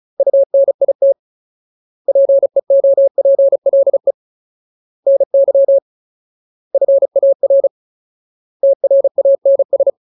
Morse code, heard as a keyed tone, about 34 wpm, 550 Hz. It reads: UNIT PEOPLE NY FAR TRANS
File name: Ultimate Word List / 35WPM Ultimate